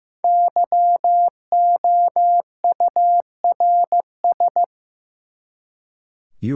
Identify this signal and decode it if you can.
Morse code, heard as a keyed tone, about 15 wpm, 700 Hz. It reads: YOURS
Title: Five Letter Words - 15wpm